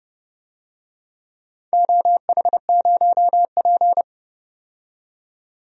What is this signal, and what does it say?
Morse code, heard as a keyed tone, about 30 wpm, 700 Hz. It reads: OH0P
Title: DXCC-100 Call Signs - 4 Characters - 30wpm